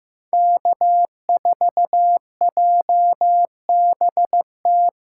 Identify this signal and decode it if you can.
Morse code, heard as a keyed tone, about 15 wpm, 700 Hz. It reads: K4JBT
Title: USA Callsign DE Callsign - 5 Characters - 15wpm